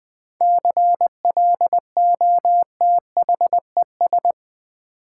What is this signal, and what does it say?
Morse code, heard as a keyed tone, about 20 wpm, 700 Hz. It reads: CLOTHES
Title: 7 Letter Words - First 3 Letters Given - 20wpm